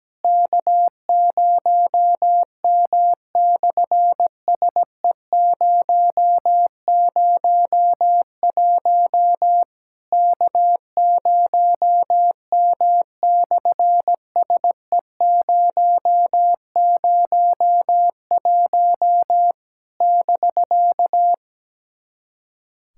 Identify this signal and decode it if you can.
Morse code, heard as a keyed tone, about 17 wpm, 700 Hz. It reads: K0M/SE001 K0M/SE001 <BK>
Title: SOTA QSO Practice 17wpm